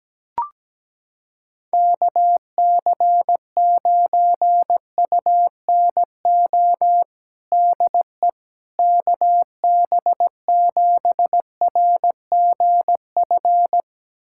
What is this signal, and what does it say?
Morse code, heard as a keyed tone, about 17 wpm, 700 Hz. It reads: KC9UNO DE KB7RGF
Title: USA Callsign DE Callsign 17wpm